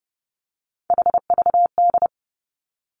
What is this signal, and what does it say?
Morse code, heard as a keyed tone, about 30 wpm, 700 Hz. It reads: HVD
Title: Three Challenging Letter-Number Combinations 30wpm